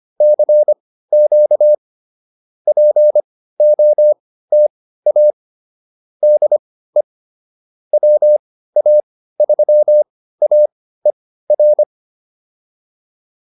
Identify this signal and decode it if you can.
Morse code, heard as a keyed tone, about 25 wpm, spaced as for 15 wpm, 600 Hz. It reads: CQ POTA DE WA3AER